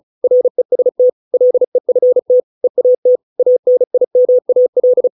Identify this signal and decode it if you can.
Morse code, heard as a keyed tone, about 34 wpm, 500 Hz. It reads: REST LEFT EAT ANIMAL